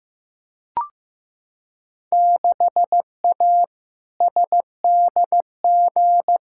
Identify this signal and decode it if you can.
Morse code, heard as a keyed tone, about 15 wpm, 700 Hz. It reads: 6A SDG
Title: Field Day Exchanges Encore - 15wpm